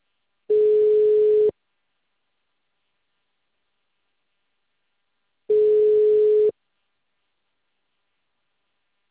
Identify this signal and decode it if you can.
Ringback tone (Europe, 425 Hz).